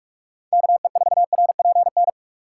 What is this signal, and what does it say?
Morse code, heard as a keyed tone, about 46 wpm, 700 Hz. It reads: KE4RPN
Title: USA Call Signs - 6x Character Spacing - 45wpm